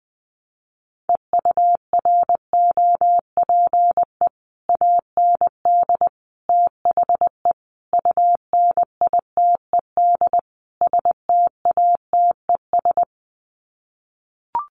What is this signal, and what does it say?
Morse code, heard as a keyed tone, about 20 wpm, 700 Hz. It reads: EUROPE AND THE UNITED STATES
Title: Sets of 5 Words Encore2 20wpm